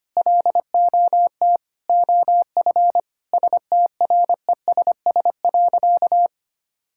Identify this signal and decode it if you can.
Morse code, heard as a keyed tone, about 25 wpm, 700 Hz. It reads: LOT OF STRESS.
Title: Sentences from Top 1000 Words - Review Part 6 - 25wpm